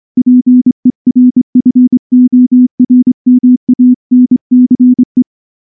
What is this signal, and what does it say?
Morse code, heard as a keyed tone, about 24 wpm, 250 Hz. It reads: PERFORMANCE
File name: morse / morse-training-25WPM